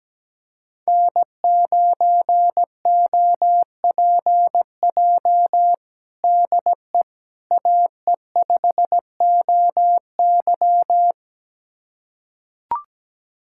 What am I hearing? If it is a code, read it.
Morse code, heard as a keyed tone, about 17 wpm, 700 Hz. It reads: N9OPJ DE AE5OY
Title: USA Callsign DE Callsign - 5 Characters - 17wpm